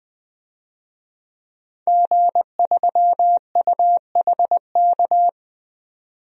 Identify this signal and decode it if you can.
Morse code, heard as a keyed tone, about 21 wpm, 700 Hz. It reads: G3UHK
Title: DXCC-100 Call Signs - 5 Characters - 20wpm